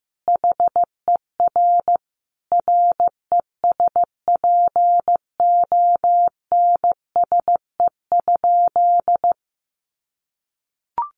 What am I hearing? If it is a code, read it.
Morse code, heard as a keyed tone, about 15 wpm, 700 Hz. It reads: HER RESPONSE?
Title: Sentences from Top 1400 Words 15wpm